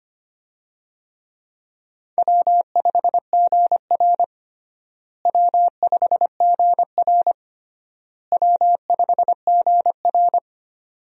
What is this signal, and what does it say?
Morse code, heard as a keyed tone, about 25 wpm, 700 Hz. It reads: W5GR W5GR W5GR
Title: USA Call Signs - 4 Characters - Repeated 3x - 25wpm